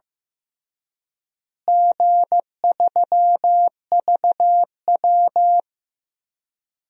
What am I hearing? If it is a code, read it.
Morse code, heard as a keyed tone, about 15 wpm, 700 Hz. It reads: G3VW
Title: DXCC-100 Call Signs - 4 Characters - Repeated 3x - 15wpm